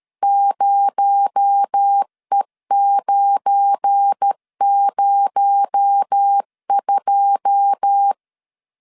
Morse code, heard as a keyed tone, about 13 wpm, 800 Hz. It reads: 0E902
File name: CW / DET / 1 9 1500C S A 12wpm 3800